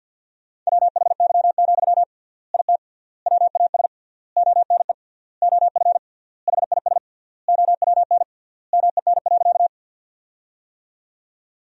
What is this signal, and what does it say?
Morse code, heard as a keyed tone, about 52 wpm, 700 Hz. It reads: WHY, IT WAS ONE OF HIS OWN MEN.
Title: 3x Word-Spacing - Sentences from Top 300 Words - 50wpm